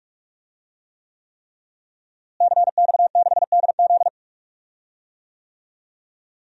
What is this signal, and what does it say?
Morse code, heard as a keyed tone, about 46 wpm, 700 Hz. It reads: CX6DZ
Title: DXCC-100 Call Signs - 5 Characters - Repeated 3x - 45wpm